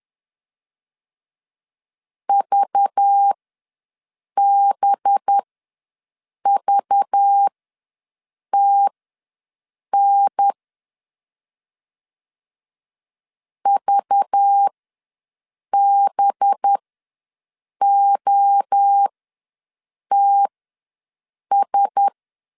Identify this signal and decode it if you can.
Morse code, heard as a keyed tone, about 10 wpm, spaced as for 6 wpm, 800 Hz. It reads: VBVTN VBOTS
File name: LEZIONI / 5 LBVC / 5 LBV ws 10wpm 3645